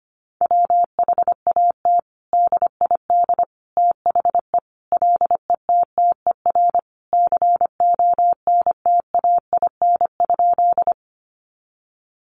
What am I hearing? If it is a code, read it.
Morse code, heard as a keyed tone, about 25 wpm, 700 Hz. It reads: WHAT DID THE LETTER CONTAIN?
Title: Sentences from Top 500 Words 25wpm